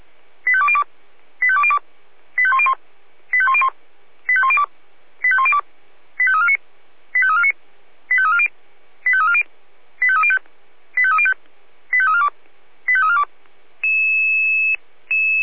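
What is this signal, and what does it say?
Five-tone selective call (ZVEI), heard as quick runs of five tones, tones of about 70 ms. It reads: 85202 85202 85101 85101 85102 85102 85349 85349 85340 85340 85305 85305 85342 85342
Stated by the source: Five-tone ZVEI selective call received by SDR.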